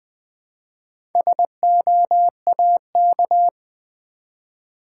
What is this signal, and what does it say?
Morse code, heard as a keyed tone, about 20 wpm, 700 Hz. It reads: SOAK
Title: Build A Word - 1 Letter At A Time - 4-Letter Words - 20wpm